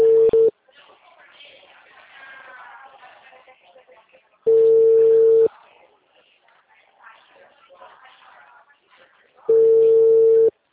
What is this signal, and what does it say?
Ringback tone (China, 450 Hz), with speech.